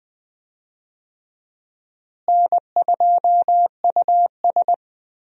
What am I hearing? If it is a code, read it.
Morse code, heard as a keyed tone, about 21 wpm, 700 Hz. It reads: N2US
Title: USA Call Signs - 4 Characters - Repeated 3x - 20wpm